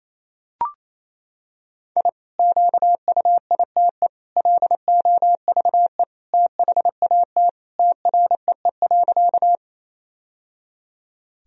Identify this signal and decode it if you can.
Morse code, heard as a keyed tone, about 28 wpm, 700 Hz. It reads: I QUITE LOVE THAT TREE.
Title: Sentences from Top 1000 Words - Review Part 1 - 28wpm